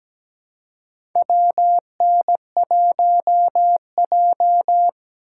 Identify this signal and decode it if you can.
Morse code, heard as a keyed tone, about 17 wpm, 700 Hz. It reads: WN1J
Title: USA Call Signs - 6x Character Spacing - 17wpm